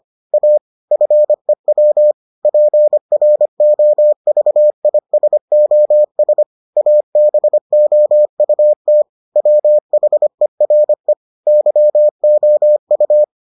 Morse code, heard as a keyed tone, about 25 wpm, 600 Hz. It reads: A FEW PROVISOS ABOUT WHERE YOU